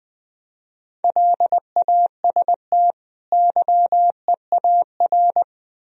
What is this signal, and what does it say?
Morse code, heard as a keyed tone, about 23 wpm, 700 Hz. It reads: LAST YEAR
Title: Speed-Racing - Sets of 2 Words 20wpm